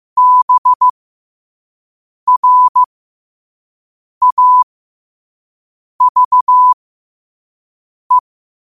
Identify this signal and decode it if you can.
Morse code, heard as a keyed tone, about 15 wpm, spaced as for 5 wpm, 1000 Hz. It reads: BRAVE
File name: qotdfiles / 5wpm